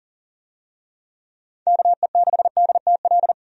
Morse code, heard as a keyed tone, about 40 wpm, 700 Hz. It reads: KE6DTL
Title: USA Call Signs - 6 Characters - 40wpm